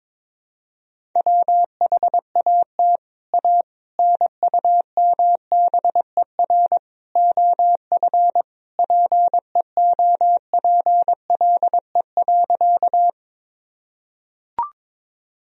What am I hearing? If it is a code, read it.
Morse code, heard as a keyed tone, about 22 wpm, 700 Hz. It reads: WHAT A NUMBER OF PEOPLE.
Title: Sentences from Top 1000 Words - Review Part 1 - 22wpm